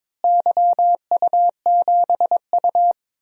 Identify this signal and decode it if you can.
Morse code, heard as a keyed tone, about 22 wpm, 700 Hz. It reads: YU7U
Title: DXCC-100 Call Signs - 4 Characters - Repeated 3x - 22wpm